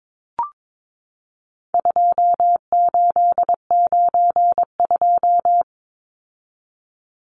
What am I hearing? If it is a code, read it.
Morse code, heard as a keyed tone, about 22 wpm, 700 Hz. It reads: 2892
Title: Numbers - 4-Digit 22wpm